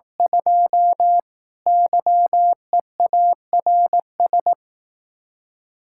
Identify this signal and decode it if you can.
Morse code, heard as a keyed tone, about 18 wpm, 700 Hz. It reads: 2 YEARS
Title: Speed-Racing Common QSO Phrases Encore - 15wpm